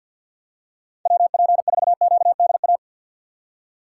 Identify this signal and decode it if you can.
Morse code, heard as a keyed tone, about 46 wpm, 700 Hz. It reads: WP4QDA
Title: USA Call Signs - 3x Character Spacing - 50wpm